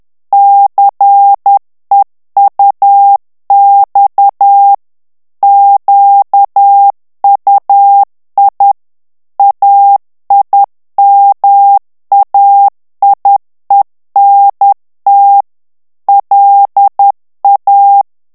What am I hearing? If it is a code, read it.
Morse code, heard as a keyed tone, about 10 wpm, 800 Hz. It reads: CEUX QUI AIMAIENT LA
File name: morse mp3 / FRANCAIS TEXTE CLAIR 2 VITESSE 10 MOTS MINUTES